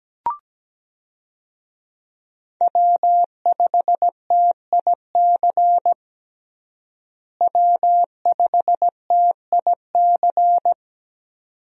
Morse code, heard as a keyed tone, about 17 wpm, 700 Hz. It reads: W5TIC W5TIC
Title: USA Call Signs - 5 Characters - Repeated 2x - 17wpm